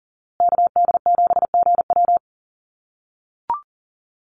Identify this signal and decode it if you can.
Morse code, heard as a keyed tone, about 40 wpm, 700 Hz. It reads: KD7GW
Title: USA Call Signs Repeated 2x 40wpm